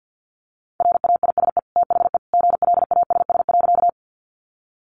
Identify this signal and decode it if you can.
Morse code, heard as a keyed tone, about 50 wpm, 700 Hz. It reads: RAISE THE GLASS.
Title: Sentences from Top 1400 Words 50wpm